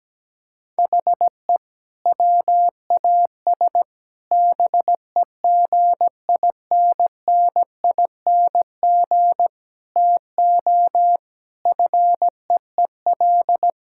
Morse code, heard as a keyed tone, about 17 wpm, 700 Hz. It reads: HE WAS BEGINNING TO FEEL
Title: Sentences from Top 1000 Words - Review Part 5 - 17wpm